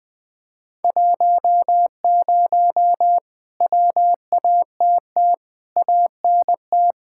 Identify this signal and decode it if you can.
Morse code, heard as a keyed tone, about 20 wpm, 700 Hz. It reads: 10 WATT ANT
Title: Speed-Racing Common QSO Phrases - 20wpm